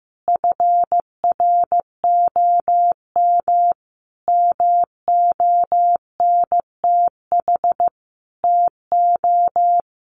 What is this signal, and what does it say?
Morse code, heard as a keyed tone, about 15 wpm, 700 Hz. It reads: FROM MONTH TO
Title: Sentences from Top 1200 Words 15wpm